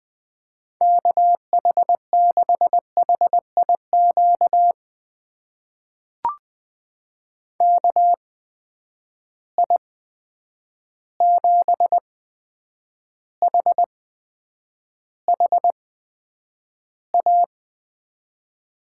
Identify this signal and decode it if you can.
Morse code, heard as a keyed tone, about 20 wpm, spaced as for 9 wpm, 700 Hz. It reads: KH6HIQ KI7HHA
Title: USA Call Signs - 6 Characters - 8x Character Spacing - 20wpm